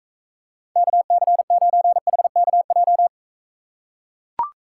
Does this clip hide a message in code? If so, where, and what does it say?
Yes: Morse code from 0.8 to 3.1 s, heard as a keyed tone, about 42 wpm, 700 Hz. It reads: KC9HKJ